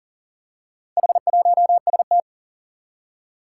Morse code, heard as a keyed tone, about 40 wpm, 700 Hz. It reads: H1ST
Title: Four Challenging Letter-Number Combinations 40wpm